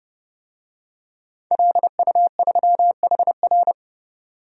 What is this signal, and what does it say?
Morse code, heard as a keyed tone, about 30 wpm, 700 Hz. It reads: LU3HR
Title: DXCC-100 Call Signs - 5 Characters - 30wpm